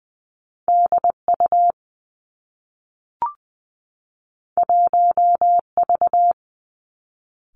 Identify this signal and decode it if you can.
Morse code, heard as a keyed tone, about 20 wpm, 700 Hz. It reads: D U 1 V
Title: Two Challenging Letter-Number Combinations 20wpm